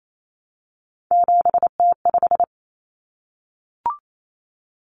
Morse code, heard as a keyed tone, about 28 wpm, 700 Hz. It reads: 7T5
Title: Common Cut Numbers - 3-Digit 28wpm